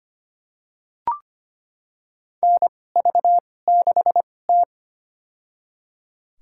Morse code, heard as a keyed tone, about 25 wpm, spaced as for 18 wpm, 700 Hz. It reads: NV6T
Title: USA Call Signs - 2x Character Spacing - 25wpm